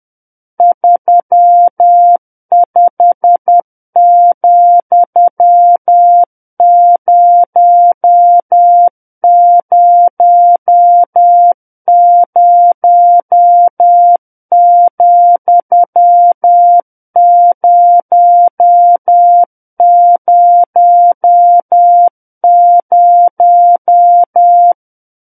Morse code, heard as a keyed tone, about 10 wpm, 700 Hz. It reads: 35,000,000
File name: Book One / 01-Chapter 1 - 10 WPM